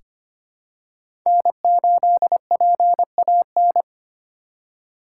Morse code, heard as a keyed tone, about 25 wpm, 700 Hz. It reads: N8PAN
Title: USA Call Signs - 3x Character Spacing - 25wpm